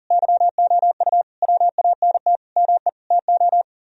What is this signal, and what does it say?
Morse code, heard as a keyed tone, about 50 wpm, 700 Hz. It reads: YOU WANT ME TO